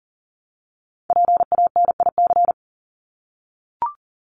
Morse code, heard as a keyed tone, about 40 wpm, 700 Hz. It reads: PANIC
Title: Top 1751-2000 Words 40wpm